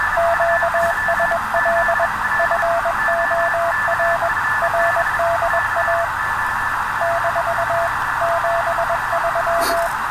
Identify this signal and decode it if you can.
Morse code, heard as a keyed tone, about 21 wpm, 680 Hz, with radio noise. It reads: QSL FOR RDA - 73